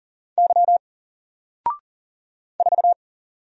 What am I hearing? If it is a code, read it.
Morse code, heard as a keyed tone, about 40 wpm, 700 Hz. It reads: Y 4
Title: Single Letter-Number 40wpm